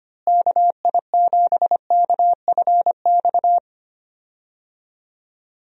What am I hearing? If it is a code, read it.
Morse code, heard as a keyed tone, about 25 wpm, 700 Hz. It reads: KI7KFX
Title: USA Call Signs - 6 Characters - Repeated 3x - 25wpm